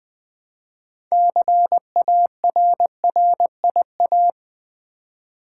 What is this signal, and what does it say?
Morse code, heard as a keyed tone, about 20 wpm, 700 Hz. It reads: CARRIA
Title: Build A Word - 3 Letters At A Time - 8-Letter Words - 20wpm